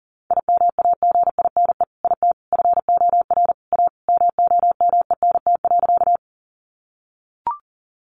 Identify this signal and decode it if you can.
Morse code, heard as a keyed tone, about 38 wpm, 700 Hz. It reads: IMAGINE IT FOR A MOMENT.
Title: Sentences from Top 1400 Words 40wpm